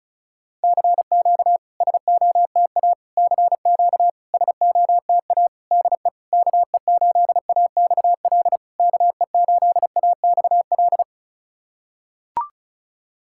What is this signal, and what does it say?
Morse code, heard as a keyed tone, about 36 wpm, 690 Hz. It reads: CQ SOTA CQ SOTA DE KE8AXL KE8AXL
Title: SOTA QSO Practice 35wpm